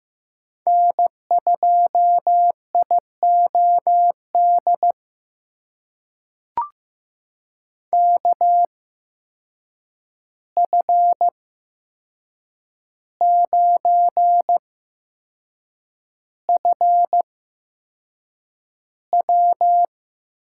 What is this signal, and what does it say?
Morse code, heard as a keyed tone, about 15 wpm, spaced as for 6 wpm, 700 Hz. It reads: N2IOD KF9FW
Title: USA Call Signs - 5 Characters - 8x Character Spacing - 15wpm